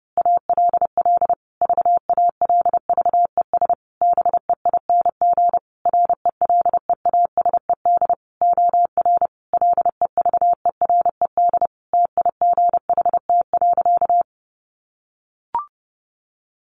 Morse code, heard as a keyed tone, about 30 wpm, 700 Hz. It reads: ALL VALVES BEING RELEASED OR LEVERED TIGHT.
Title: A Sound Of Thunder 30wpm